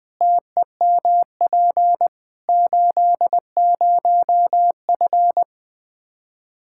Morse code, heard as a keyed tone, about 20 wpm, 700 Hz. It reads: TEMP 80F
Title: K1USN SST QSO Practice 20wpm